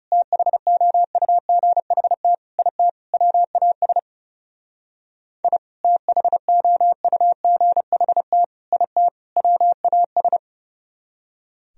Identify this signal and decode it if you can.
Morse code, heard as a keyed tone, about 35 wpm, 700 Hz. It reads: THOUGHT IT WAS I THOUGHT IT WAS
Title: Speed-Racing - Sets of 4 Words 30wpm